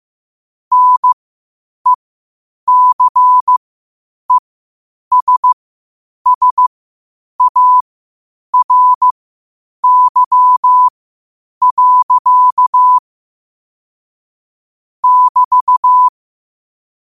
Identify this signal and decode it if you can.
Morse code, heard as a keyed tone, about 15 wpm, spaced as for 8 wpm, 1000 Hz. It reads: NECESSARY. <BT>
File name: qotdfiles / 8wpm